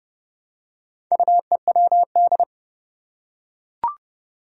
Morse code, heard as a keyed tone, about 29 wpm, 700 Hz. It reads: UEWD
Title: Four Challenging Letter-Number Combinations 30wpm